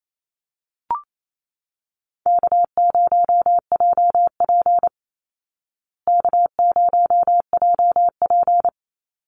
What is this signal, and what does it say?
Morse code, heard as a keyed tone, about 28 wpm, 700 Hz. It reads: K0JP K0JP
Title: Call Signs Repeated 3x - 28wpm